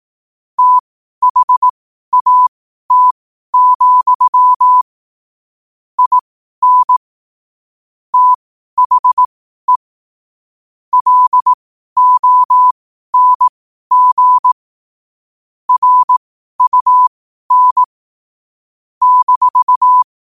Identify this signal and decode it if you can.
Morse code, heard as a keyed tone, about 18 wpm, spaced as for 12 wpm, 1000 Hz. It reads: THAT, IN THE LONG RUN -
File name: qotdfiles / 12wpm